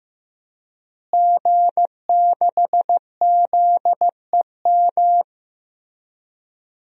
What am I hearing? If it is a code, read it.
Morse code, heard as a keyed tone, about 15 wpm, 700 Hz. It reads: G6ZEM
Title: DXCC-100 Call Signs - 5 Characters - Repeated 3x - 15wpm